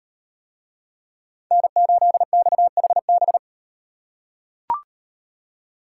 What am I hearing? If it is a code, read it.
Morse code, heard as a keyed tone, about 38 wpm, 700 Hz. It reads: N8XHB